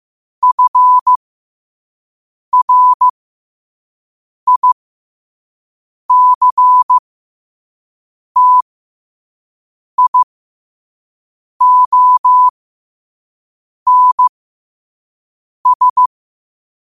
Morse code, heard as a keyed tone, about 15 wpm, spaced as for 5 wpm, 1000 Hz. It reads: FRICTIONS
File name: qotdfiles / 5wpm